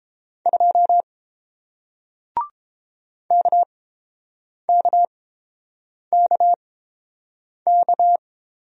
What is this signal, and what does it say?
Morse code, heard as a keyed tone, about 30 wpm, spaced as for 7 wpm, 700 Hz. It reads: 2 KKKK